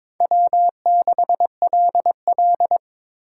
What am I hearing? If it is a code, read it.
Morse code, heard as a keyed tone, about 22 wpm, 700 Hz. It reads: W6LL
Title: USA Call Signs - 4 Characters - 2x Character Spacing - 22wpm